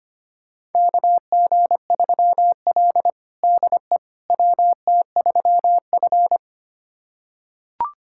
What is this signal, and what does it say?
Morse code, heard as a keyed tone, about 25 wpm, 700 Hz. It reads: KG3L DE WT3F
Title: USA Callsign DE Callsign - 4 Characters - 25wpm